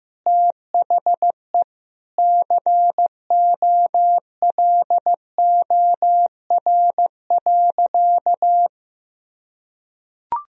Morse code, heard as a keyed tone, about 15 wpm, 700 Hz. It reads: THE COLOR.
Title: Sentences from Top 1000 Words - Review Part 4 - 15wpm